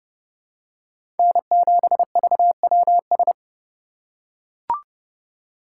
Morse code, heard as a keyed tone, about 31 wpm, 700 Hz. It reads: N7VWS
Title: USA Call Signs - 4x Character Spacing - 30wpm